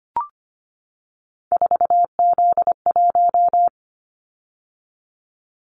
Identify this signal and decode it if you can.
Morse code, heard as a keyed tone, about 25 wpm, 700 Hz. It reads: 4Z1
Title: Callsign Prefixes for Top 50 DXCC - 25wpm